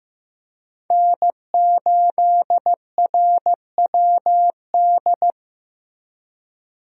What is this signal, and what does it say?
Morse code, heard as a keyed tone, about 16 wpm, 700 Hz. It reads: N8RWD
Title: USA Call Signs - 5 Characters - Repeated 2x - 15wpm